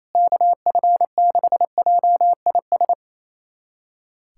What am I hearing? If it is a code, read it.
Morse code, heard as a keyed tone, about 28 wpm, 700 Hz. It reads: KF6JIS
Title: USA Callsign DE Callsign - 6 Characters - 28wpm